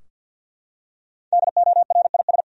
Morse code, heard as a keyed tone, about 47 wpm, 700 Hz. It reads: DORIS
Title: USA Common Names - Max 5 Letters - 50wpm